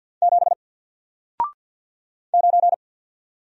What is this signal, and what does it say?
Morse code, heard as a keyed tone, about 48 wpm, 700 Hz. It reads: 7 9
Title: Single Character (Letter:Number:Punctuation:Prosign) 50wpm